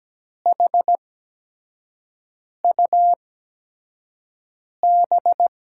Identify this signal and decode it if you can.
Morse code, heard as a keyed tone, about 17 wpm, spaced as for 5 wpm, 700 Hz. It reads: HUB
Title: USA Call Signs - 6 Characters - 8x Character Spacing - 17wpm